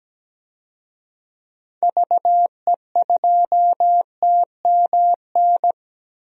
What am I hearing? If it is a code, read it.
Morse code, heard as a keyed tone, about 17 wpm, 700 Hz. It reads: VE2TMN
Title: DXCC-100 Call Signs - 6 Characters - Repeated 2x - 17wpm